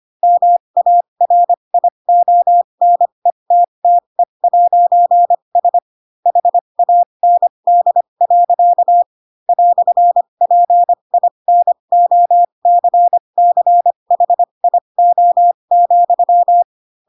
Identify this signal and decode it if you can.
Morse code, heard as a keyed tone, about 25 wpm, spaced as for 22 wpm, 700 Hz. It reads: MARIONETTE'S HAND. "PINOCCHIO,
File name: Pinocchio / Pinocchio ch3 18 25 ik1hge